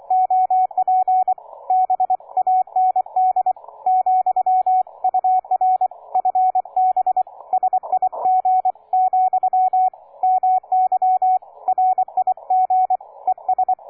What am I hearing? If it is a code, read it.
Morse code, heard as a keyed tone, about 24 wpm, 740 Hz, with radio noise. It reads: OP B AND , UR FB SIG , MY RIG ES
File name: cw forum / K9DX 1818 le 17-12-2008 05h47